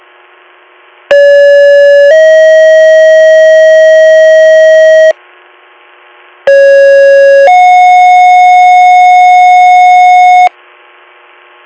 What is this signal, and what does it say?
Two-tone page, heard as a one-second tone, then a three-second tone, about 570 then 640 Hz, about 550 then 730 Hz. It is heard from 1.1 to 10.5 s.